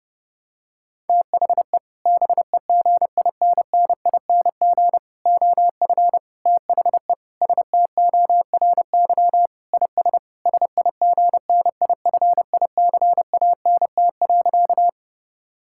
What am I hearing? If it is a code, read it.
Morse code, heard as a keyed tone, about 31 wpm, 700 Hz. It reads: THE BEGINNING OF THE STORY IS SIGNIFICANT.
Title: Sentences from Top 1000 Words - Review Part 6 - 30wpm